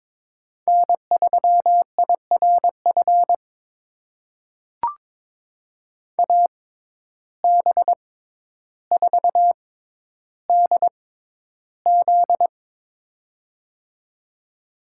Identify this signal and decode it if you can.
Morse code, heard as a keyed tone, about 23 wpm, spaced as for 11 wpm, 700 Hz. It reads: N3IRF AB4DZ
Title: USA Call Signs - 5 Characters - 6x Character Spacing - 22wpm